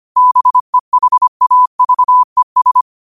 Morse code, heard as a keyed tone, about 25 wpm, 1000 Hz. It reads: BEHAVES
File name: qotdfiles / 25wpm